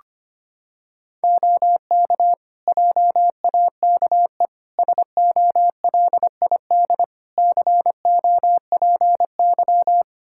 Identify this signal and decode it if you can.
Morse code, heard as a keyed tone, about 25 wpm, 700 Hz. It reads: OK JAKE SOLID COPY
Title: Practice QSO - Exchange 3 - More Info - 25wpm